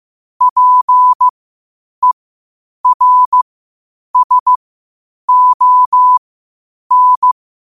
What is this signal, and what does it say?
Morse code, heard as a keyed tone, about 15 wpm, spaced as for 8 wpm, 1000 Hz. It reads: PERSON
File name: qotdfiles / 8wpm